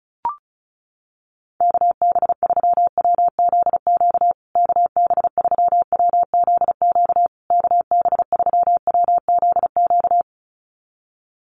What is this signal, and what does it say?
Morse code, heard as a keyed tone, about 34 wpm, 700 Hz. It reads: KB3WZQ KB3WZQ KB3WZQ
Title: USA Call Signs Repeated 3x 35wpm